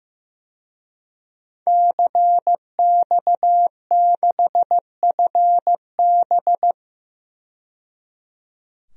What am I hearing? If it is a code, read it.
Morse code, heard as a keyed tone, about 15 wpm, 700 Hz. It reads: CX6FB
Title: DXCC-100 Call Signs - 5 Characters - 15wpm